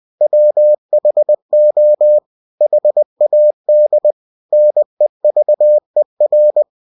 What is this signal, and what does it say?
Morse code, heard as a keyed tone, about 20 wpm, 600 Hz. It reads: WHO HAD NEVER
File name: cw-books / winnie-the-pooh / winnie-20wpm-page-0071